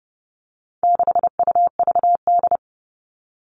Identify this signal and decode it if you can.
Morse code, heard as a keyed tone, about 30 wpm, 700 Hz. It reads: 6UVD
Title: All Cut Numbers - 4-Digit 30wpm